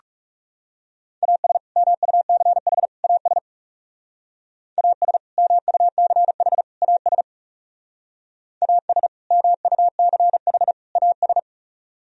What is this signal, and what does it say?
Morse code, heard as a keyed tone, about 38 wpm, 700 Hz. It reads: AS MUCH AS AS MUCH AS AS MUCH AS